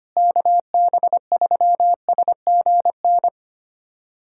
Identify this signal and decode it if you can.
Morse code, heard as a keyed tone, about 25 wpm, 700 Hz. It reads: KB3SGN
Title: USA Callsign DE Callsign 25wpm